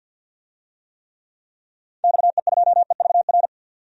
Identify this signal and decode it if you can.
Morse code, heard as a keyed tone, about 51 wpm, 700 Hz. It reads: XE2EVR
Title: DXCC-100 Call Signs - 6 Characters - 50wpm